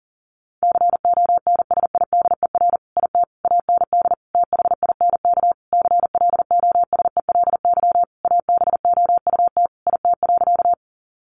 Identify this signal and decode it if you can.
Morse code, heard as a keyed tone, about 40 wpm, 700 Hz. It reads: CONSIDER IT AND THINK CLOSELY ABOUT IT.
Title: Sentences from Top 1400 Words 40wpm